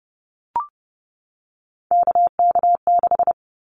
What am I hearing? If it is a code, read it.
Morse code, heard as a keyed tone, about 30 wpm, 700 Hz. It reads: KK6
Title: Callsign Prefixes for Top 25 DXCC - 30wpm